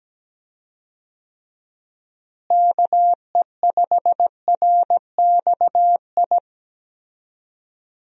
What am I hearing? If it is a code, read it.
Morse code, heard as a keyed tone, about 17 wpm, 700 Hz. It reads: KE5RXI
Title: USA Call Signs - 6 Characters - Repeated 3x - 17wpm